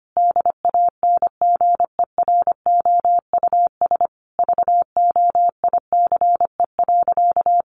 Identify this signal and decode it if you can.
Morse code, heard as a keyed tone, about 25 wpm, 700 Hz. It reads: DANGEROUS VOICE.
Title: Sentences from Top 1500 Words 25wpm